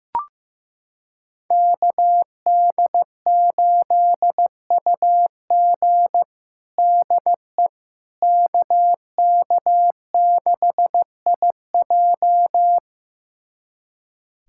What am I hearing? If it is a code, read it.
Morse code, heard as a keyed tone, about 15 wpm, 700 Hz. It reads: KD8UG DE KK6IJ
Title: USA Callsign DE Callsign - 5 Characters - 15wpm